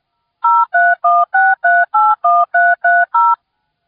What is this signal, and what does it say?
DTMF, heard as telephone touch tones, tones of about 220 ms. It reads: 0316381330